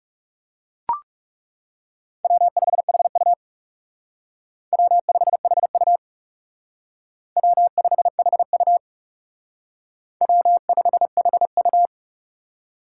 Morse code, heard as a keyed tone, about 34 wpm, 700 Hz. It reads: W5HU W5HU W5HU W5HU
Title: Speed Racing - USA 1x2 Call Signs 30wpm